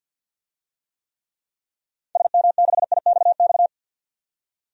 Morse code, heard as a keyed tone, about 49 wpm, 700 Hz. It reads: SM6IXX
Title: DXCC-100 Call Signs - 6 Characters - 50wpm